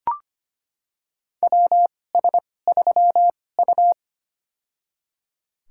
Morse code, heard as a keyed tone, about 25 wpm, spaced as for 18 wpm, 700 Hz. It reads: WS3U
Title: USA Call Signs - 4 Characters - 2x Character Spacing - 25wpm